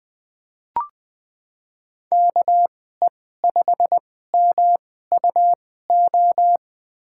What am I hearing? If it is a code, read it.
Morse code, heard as a keyed tone, about 20 wpm, spaced as for 14 wpm, 700 Hz. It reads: KE5MUO
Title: USA Call Signs - 6 Characters - 2x Character Spacing - 20wpm